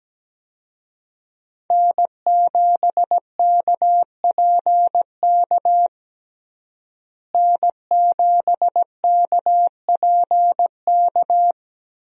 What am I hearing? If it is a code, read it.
Morse code, heard as a keyed tone, about 17 wpm, 700 Hz. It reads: N7KPK N7KPK
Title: USA Call Signs - 5 Characters - Repeated 2x - 17wpm